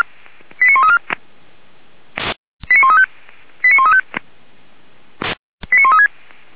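Five-tone selective call (ZVEI), heard as quick runs of five tones, tones of about 80 ms. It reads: 89134 89135 89135 89136
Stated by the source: ZVEI five-tone selective call received on SDR, from Feuerwehr Orlamünde fire brigade server.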